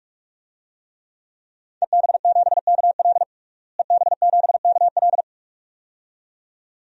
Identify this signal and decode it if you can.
Morse code, heard as a keyed tone, about 46 wpm, 700 Hz. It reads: EB7KL EB7KL
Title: DXCC-100 Call Signs - 5 Characters - Repeated 2x - 45wpm